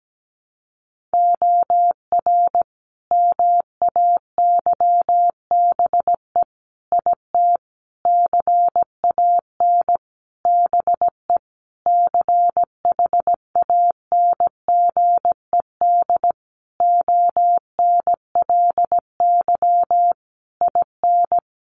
Morse code, heard as a keyed tone, about 17 wpm, 700 Hz. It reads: OR MAYBE IT CAN BE CHANGED ONLY IN
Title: A Sound Of Thunder 17wpm